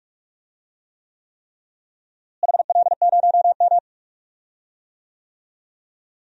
Morse code, heard as a keyed tone, about 45 wpm, 700 Hz. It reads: HL0M
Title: DXCC-100 Call Signs - 4 Characters - Repeated 3x - 45wpm